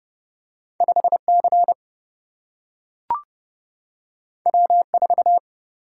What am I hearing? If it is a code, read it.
Morse code, heard as a keyed tone, about 30 wpm, 700 Hz. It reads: 5 C W 4
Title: Single-Letter Single-Number 30wpm